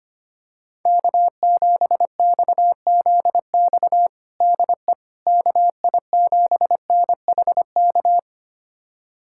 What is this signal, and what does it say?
Morse code, heard as a keyed tone, about 25 wpm, 700 Hz. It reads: K7XZX DE KI7NHK
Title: USA Callsign DE Callsign 25wpm